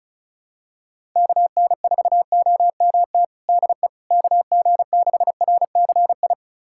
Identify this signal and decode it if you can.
Morse code, heard as a keyed tone, about 36 wpm, 700 Hz. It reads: KN4OMT DE KG6RCI
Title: USA Callsign DE Callsign - 6 Characters - 35wpm